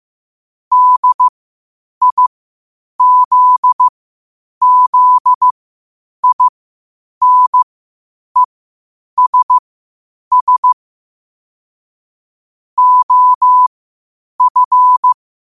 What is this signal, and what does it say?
Morse code, heard as a keyed tone, about 15 wpm, spaced as for 8 wpm, 1000 Hz. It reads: DIZZINESS OF
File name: qotdfiles / 8wpm